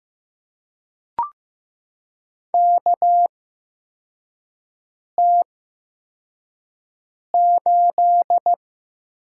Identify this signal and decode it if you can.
Morse code, heard as a keyed tone, about 15 wpm, spaced as for 4 wpm, 700 Hz. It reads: KT8
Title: USA Call Signs - 4 Characters - 8x Character Spacing - 15wpm